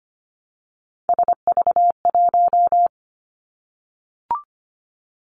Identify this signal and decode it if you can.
Morse code, heard as a keyed tone, about 25 wpm, 700 Hz. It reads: SV1
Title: Callsign Prefixes for Top 50 DXCC - 25wpm